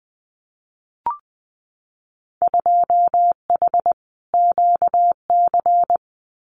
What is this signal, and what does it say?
Morse code, heard as a keyed tone, about 20 wpm, 700 Hz. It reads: 2H QC
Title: Winter Field Day Exchanges Encore 20wpm